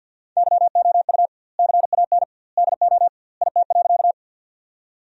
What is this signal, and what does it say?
Morse code, heard as a keyed tone, about 52 wpm, 700 Hz. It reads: YOU CAN DO IT.
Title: 2x Word-Spacing - Sentences from Top 100 Words - 50wpm